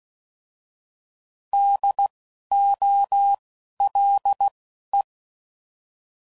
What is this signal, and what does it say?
Morse code, heard as a keyed tone, about 16 wpm, spaced as for 11 wpm, 800 Hz. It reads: DOLE
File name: Prefixes / By First Letter / Ds